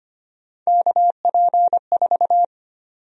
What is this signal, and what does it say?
Morse code, heard as a keyed tone, about 25 wpm, 700 Hz. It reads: KP4
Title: USA Call Sign Prefixes - 25wpm